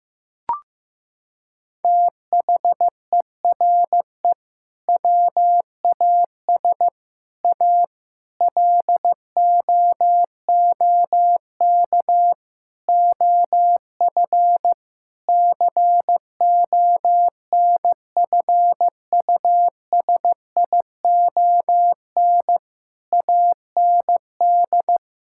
Morse code, heard as a keyed tone, about 15 wpm, 700 Hz. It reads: THERE WAS A LOOK OF CONFUSION AND
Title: Sentences from Top 1600 Words 15wpm